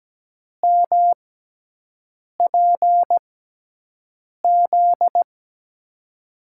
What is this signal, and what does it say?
Morse code, heard as a keyed tone, about 17 wpm, spaced as for 6 wpm, 700 Hz. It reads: MPZ
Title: USA Call Signs - 5 Characters - 6x Character Spacing - 17wpm